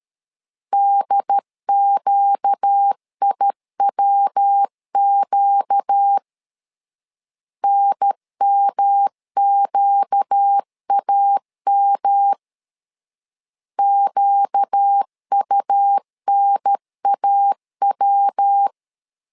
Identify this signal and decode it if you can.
Morse code, heard as a keyed tone, about 13 wpm, 800 Hz. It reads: DQIWQ NMQAM QUNAW
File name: LEZIONI / 4 Q 12wpm 1833